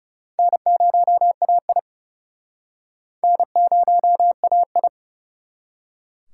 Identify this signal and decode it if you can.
Morse code, heard as a keyed tone, about 36 wpm, 700 Hz. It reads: N0AI N0AI
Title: Speed Racing - USA 1x2 Call Signs 30wpm